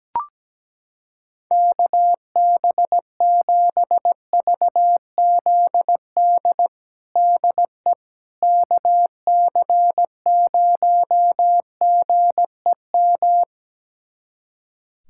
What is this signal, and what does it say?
Morse code, heard as a keyed tone, about 17 wpm, 700 Hz. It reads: KB7VZD DE KC0GEM
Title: USA Callsign DE Callsign 17wpm